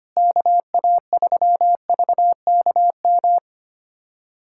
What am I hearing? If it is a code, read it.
Morse code, heard as a keyed tone, about 25 wpm, 700 Hz. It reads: KA3VKM
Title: USA Callsign DE Callsign - 6 Characters - 25wpm